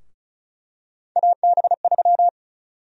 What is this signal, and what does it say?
Morse code, heard as a keyed tone, about 36 wpm, 700 Hz. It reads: AB3
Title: USA Call Sign Prefixes - Repeated 2x - 35wpm